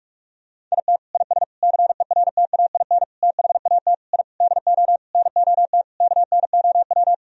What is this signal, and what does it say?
Morse code, heard as a keyed tone, about 41 wpm, 700 Hz. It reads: IT IS CERTAIN THAT I DO NOT KNOW
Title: Sentences from Top 1000 Words - Review Part 2 - 45wpm